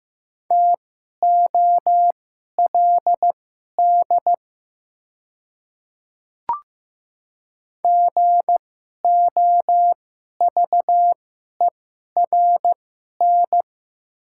Morse code, heard as a keyed tone, about 15 wpm, spaced as for 11 wpm, 700 Hz. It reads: TOLD GOVERN
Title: 2x Character Spacing - Top 500 Words - 15wpm